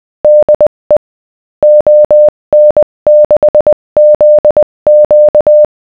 Morse code, heard as a keyed tone, about 20 wpm, 600 Hz. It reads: DE ON6ZQ
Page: The GNU Octave program below generates a sound file (.wav) with the morse code representing a given text.